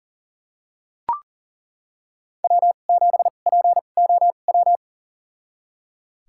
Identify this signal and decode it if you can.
Morse code, heard as a keyed tone, about 39 wpm, spaced as for 29 wpm, 700 Hz. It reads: W7POW
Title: USA Call Signs - 5 Characters - 2x Character Spacing - 40wpm